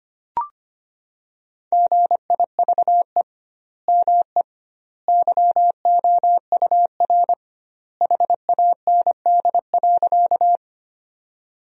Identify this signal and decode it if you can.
Morse code, heard as a keyed tone, about 25 wpm, 700 Hz. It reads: GIVE ME YOUR HAND.
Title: 2x Word-Spacing - Sentences from Top 200 Words - 25wpm